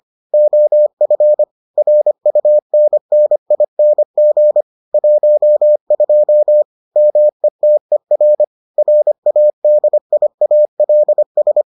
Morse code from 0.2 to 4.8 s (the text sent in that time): OF RUNNING